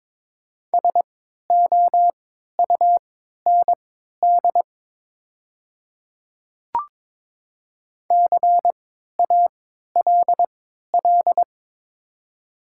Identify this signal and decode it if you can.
Morse code, heard as a keyed tone, about 22 wpm, spaced as for 13 wpm, 700 Hz. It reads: SOUND CALL
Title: 3x Character Spacing - Top 300 Words - 22wpm